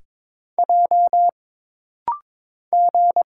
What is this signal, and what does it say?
Morse code, heard as a keyed tone, about 22 wpm, 700 Hz. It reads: J G
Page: This rapid-fire timing is geared to push your ICR (Instant Character Recognition) skills.